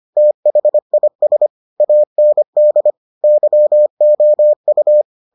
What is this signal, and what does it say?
Morse code, heard as a keyed tone, about 25 wpm, 600 Hz. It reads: THIS AND YOU